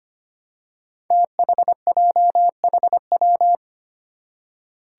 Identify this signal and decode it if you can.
Morse code, heard as a keyed tone, about 25 wpm, 700 Hz. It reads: THJHW
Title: Five Challenging Letter-Number Combinations 25wpm